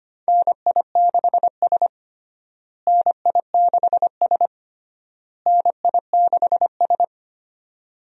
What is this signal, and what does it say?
Morse code, heard as a keyed tone, about 25 wpm, 700 Hz. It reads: NI6S NI6S NI6S
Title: USA Call Signs - 4 Characters - Repeated 3x - 25wpm